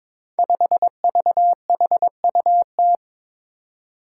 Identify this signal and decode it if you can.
Morse code, heard as a keyed tone, about 22 wpm, 700 Hz. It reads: 5VHUT
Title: Five Challenging Letter-Number Combinations 22wpm